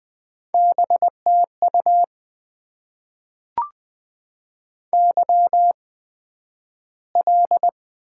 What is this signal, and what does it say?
Morse code, heard as a keyed tone, about 20 wpm, spaced as for 11 wpm, 700 Hz. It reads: BTU YL
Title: Top 90 QSO Elements - 8x Character Spacing - 20wpm